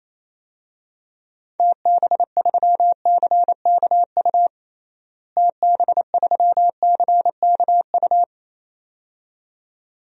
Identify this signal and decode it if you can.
Morse code, heard as a keyed tone, about 28 wpm, 700 Hz. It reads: TB3CKU TB3CKU
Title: DXCC-100 Call Signs - 6 Characters - Repeated 2x - 28wpm